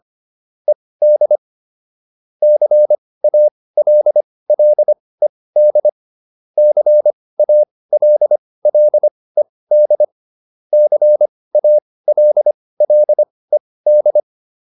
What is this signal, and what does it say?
Morse code, heard as a keyed tone, about 25 wpm, spaced as for 18 wpm, 600 Hz. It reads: ED CALLED CALLED CALLED